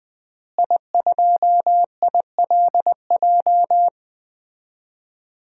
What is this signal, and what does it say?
Morse code, heard as a keyed tone, about 21 wpm, 700 Hz. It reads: I2ILJ
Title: DXCC-100 Call Signs - 5 Characters - Repeated 2x - 20wpm